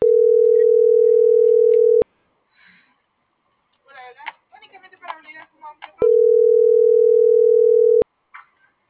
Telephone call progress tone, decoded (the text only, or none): ringback tone